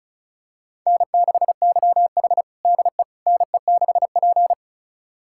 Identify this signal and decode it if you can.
Morse code, heard as a keyed tone, about 36 wpm, 700 Hz. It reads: N6YH DE NE6P
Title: USA Callsign DE Callsign - 4 Characters - 35wpm